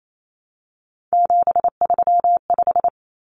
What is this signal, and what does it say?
Morse code, heard as a keyed tone, about 28 wpm, 700 Hz. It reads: 735
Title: Common Cut Numbers - 3-Digit 28wpm